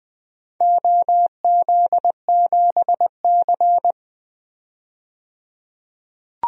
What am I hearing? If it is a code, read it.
Morse code, heard as a keyed tone, about 20 wpm, 700 Hz. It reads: OZ7C
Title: DXCC-100 Call Signs - 4 Characters - 20wpm